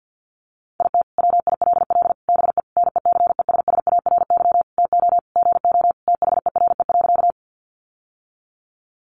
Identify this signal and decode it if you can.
Morse code, heard as a keyed tone, about 50 wpm, 700 Hz. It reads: IT WILL BE NECESSARY TO GO THERE.
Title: Sentences from Top 800 Words 50wpm